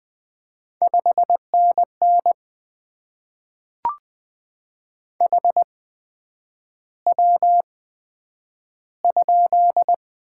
Morse code, heard as a keyed tone, about 20 wpm, spaced as for 9 wpm, 700 Hz. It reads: 5NN HW?
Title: Top 20 QSO Elements - 8x Character Spacing - 20wpm